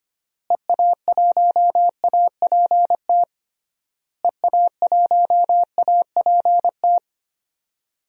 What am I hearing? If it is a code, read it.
Morse code, heard as a keyed tone, about 25 wpm, 700 Hz. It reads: EA1APT EA1APT
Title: DXCC-100 Call Signs - 6 Characters - Repeated 3x - 25wpm